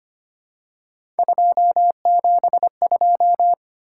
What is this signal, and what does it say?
Morse code, heard as a keyed tone, about 25 wpm, 700 Hz. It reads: 272
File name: Three Letter-Number Combinations - 25wpm